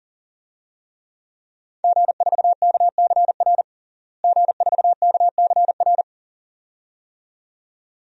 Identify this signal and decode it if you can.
Morse code, heard as a keyed tone, about 40 wpm, 690 Hz. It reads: G4KCR G4KCR
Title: DXCC-100 Call Signs - 5 Characters - Repeated 2x - 40wpm